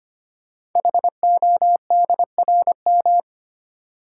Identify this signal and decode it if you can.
Morse code, heard as a keyed tone, about 25 wpm, 700 Hz. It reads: HODRM
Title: Code Groups - 5 Letters - 25wpm